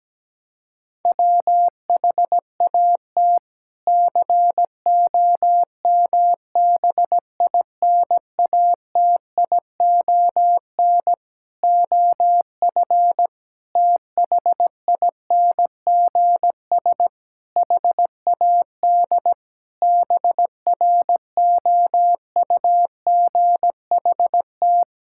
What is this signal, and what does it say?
Morse code, heard as a keyed tone, about 17 wpm, 700 Hz. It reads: WHAT COMBINATION OF THINGS HAD BROUGHT
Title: Sentences from Top 1000 Words - Review Part 6 - 17wpm